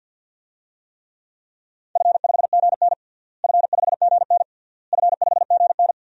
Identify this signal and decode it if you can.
Morse code, heard as a keyed tone, about 51 wpm, 700 Hz. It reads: F5GN F5GN F5GN
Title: DXCC-100 Call Signs - 4 Characters - Repeated 3x - 50wpm